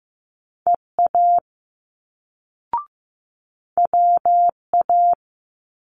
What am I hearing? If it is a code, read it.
Morse code, heard as a keyed tone, about 15 wpm, 700 Hz. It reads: EA WA
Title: Multi-letter Phonemes Quick Fire 15wpm